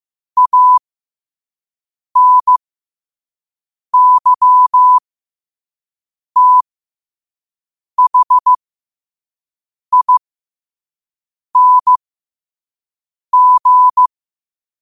Morse code, heard as a keyed tone, about 15 wpm, spaced as for 5 wpm, 1000 Hz. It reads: ANYTHING
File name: qotdfiles / 5wpm